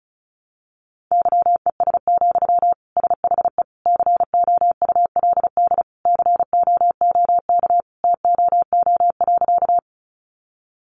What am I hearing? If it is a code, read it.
Morse code, heard as a keyed tone, about 34 wpm, 700 Hz. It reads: YES, SHE COULD COOK TOO.
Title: Sentences from Top 700 Words 35wpm